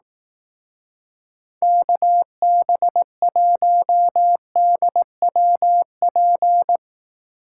Morse code, heard as a keyed tone, about 18 wpm, 700 Hz. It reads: KB1DWP
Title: Speed Racing - USA Call Signs 15wpm